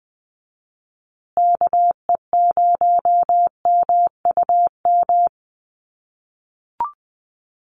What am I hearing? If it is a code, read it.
Morse code, heard as a keyed tone, about 20 wpm, 700 Hz. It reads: KE0MUM
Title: USA Call Signs Repeated 3x 20wpm